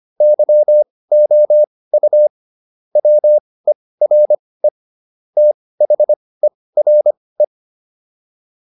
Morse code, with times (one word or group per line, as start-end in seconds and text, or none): 0.2-2.3 YOU
3.0-4.7 WERE
5.4-7.4 THERE